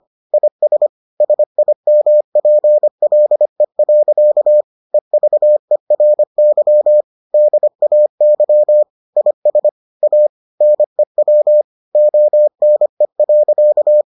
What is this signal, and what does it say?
Morse code, heard as a keyed tone, about 25 wpm, 600 Hz. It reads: IS SIMPLE. EVERY DAY IS A NEW ONE.